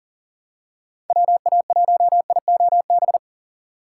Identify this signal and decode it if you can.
Morse code, heard as a keyed tone, about 40 wpm, 700 Hz. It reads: WA1IOB
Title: USA Call Signs - 4x Character Spacing - 40wpm